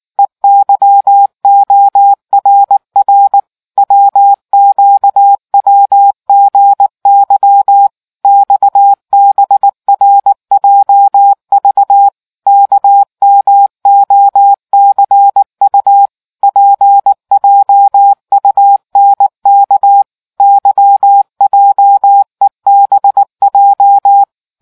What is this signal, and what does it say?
Morse code, heard as a keyed tone, about 18 wpm, 800 Hz. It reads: EYORR WQWGY XBRJV KMOCU PJUNK YJEBJ